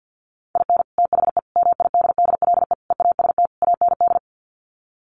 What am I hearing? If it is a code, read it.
Morse code, heard as a keyed tone, about 50 wpm, 700 Hz. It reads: IN THE MIDDLE EAST AND
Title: Sets of 5 Words Encore 50wpm